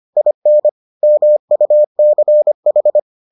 Morse code, heard as a keyed tone, about 25 wpm, 600 Hz. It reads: IN MUCH